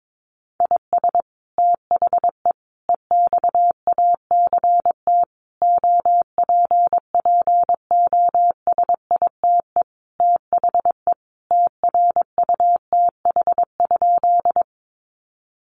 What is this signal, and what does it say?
Morse code, heard as a keyed tone, about 22 wpm, 700 Hz. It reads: IS THE EXACT OPPOSITE THE TRUTH?
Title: Sentences from Top 1300 Words 22wpm